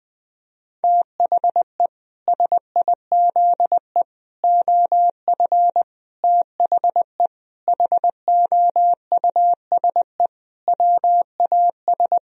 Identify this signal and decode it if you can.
Morse code, heard as a keyed tone, about 21 wpm, 700 Hz. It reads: THE SIZE OF THE HOUSE WAS
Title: Sentences from Top 1000 Words - Review Part 2 - 20wpm